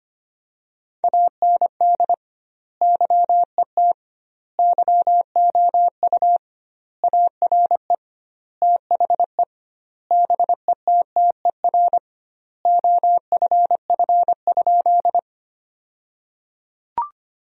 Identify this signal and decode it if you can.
Morse code, heard as a keyed tone, about 25 wpm, 700 Hz. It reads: AND YET YOU ARE THE BETTER OFF?
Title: 2x Word-Spacing - Sentences from Top 500 Words - 25wpm